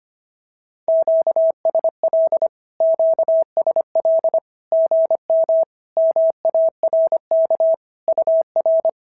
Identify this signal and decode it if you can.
Morse code, heard as a keyed tone, about 25 wpm, 690 Hz. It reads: QSL QSL GM MARK UR
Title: SOTA QSO Practice 25wpm